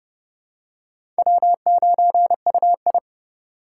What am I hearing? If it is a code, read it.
Morse code, heard as a keyed tone, about 31 wpm, 700 Hz. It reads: W9UI
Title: USA Call Signs - 4 Characters - 3x Character Spacing - 30wpm